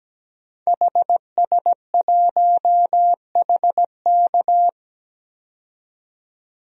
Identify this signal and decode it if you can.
Morse code, heard as a keyed tone, about 17 wpm, 700 Hz. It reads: HS1HK
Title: DXCC-100 Call Signs - 5 Characters - 17wpm